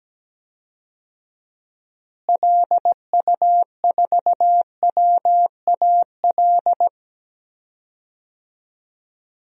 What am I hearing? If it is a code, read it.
Morse code, heard as a keyed tone, about 17 wpm, 700 Hz. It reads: LU4WAL
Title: DXCC-100 Call Signs - 6 Characters - Repeated 3x - 17wpm